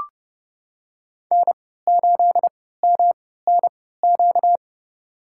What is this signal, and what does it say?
Morse code, heard as a keyed tone, about 32 wpm, spaced as for 17 wpm, 700 Hz. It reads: N8MNQ
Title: USA Call Signs - 3x Character Spacing - 30wpm